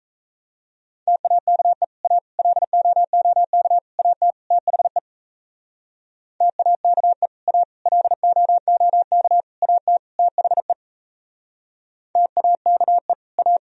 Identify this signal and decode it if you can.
Morse code, heard as a keyed tone, about 39 wpm, 700 Hz. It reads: TAKE A LOOK AT THE TAKE A LOOK AT THE TAKE A